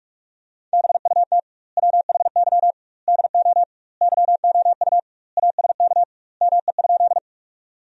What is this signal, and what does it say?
Morse code, heard as a keyed tone, about 41 wpm, 700 Hz. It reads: BUT WHY DO YOU ASK ME?
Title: 2x Word-Spacing - Sentences from Top 200 Words - 45wpm